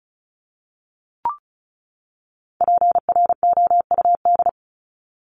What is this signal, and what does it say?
Morse code, heard as a keyed tone, about 34 wpm, 700 Hz. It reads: PROUD
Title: Top 1501-1750 Words 35wpm